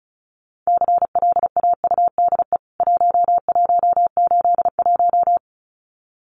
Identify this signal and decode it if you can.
Morse code, heard as a keyed tone, about 34 wpm, 700 Hz. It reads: CLAUDE 1181
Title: CWT Exchange 35wpm